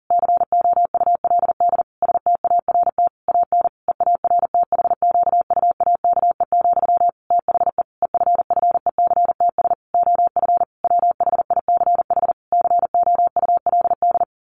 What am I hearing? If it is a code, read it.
Morse code, heard as a keyed tone, about 40 wpm, 700 Hz. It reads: COULD START AN EARTHQUAKE, THE EFFECTS OF WHICH COULD
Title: A Sound Of Thunder 40wpm